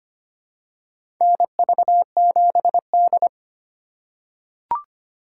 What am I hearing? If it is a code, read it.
Morse code, heard as a keyed tone, about 25 wpm, 700 Hz. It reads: NV7D
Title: USA Call Signs - 4 Characters - 4x Character Spacing - 25wpm